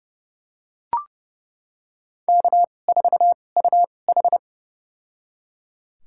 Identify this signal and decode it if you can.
Morse code, heard as a keyed tone, about 30 wpm, spaced as for 22 wpm, 700 Hz. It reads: K4UH
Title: USA Call Signs - 4 Characters - 2x Character Spacing - 30wpm